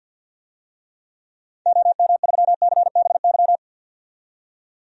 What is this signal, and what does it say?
Morse code, heard as a keyed tone, about 51 wpm, 700 Hz. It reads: OM3CBY
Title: DXCC-100 Call Signs - 6 Characters - 50wpm